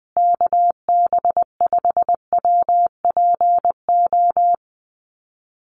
Morse code, heard as a keyed tone, about 20 wpm, 700 Hz. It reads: KB5WPO
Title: USA Call Signs Repeated 2x 20wpm